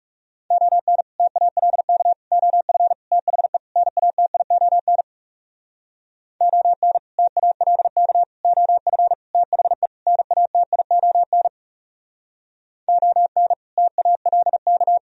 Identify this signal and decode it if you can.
Morse code, heard as a keyed tone, about 37 wpm, 700 Hz. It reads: ON TALK OF THE NATION ON TALK OF THE NATION ON TALK
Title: Speed-Racing - Sets of 5 Words 30wpm